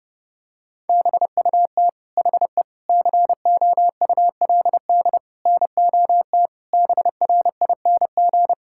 Morse code, heard as a keyed tone, about 30 wpm, 700 Hz. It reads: BUT HE COULD NOT BRING
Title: Sentences from Top 1600 Words 30wpm